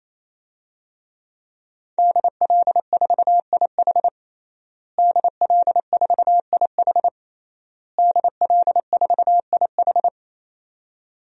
Morse code, heard as a keyed tone, about 27 wpm, 700 Hz. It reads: DL4IH DL4IH DL4IH
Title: DXCC-100 Call Signs - 5 Characters - Repeated 3x - 28wpm